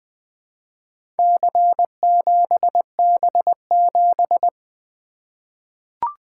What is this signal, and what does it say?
Morse code, heard as a keyed tone, about 20 wpm, 700 Hz. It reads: C7B7
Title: Four Letter-Number Combinations - 20wpm